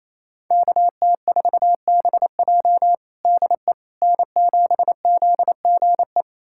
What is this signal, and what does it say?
Morse code, heard as a keyed tone, about 28 wpm, 700 Hz. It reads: KT4BJ DE N7ZGE
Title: USA Callsign DE Callsign - 5 Characters - 28wpm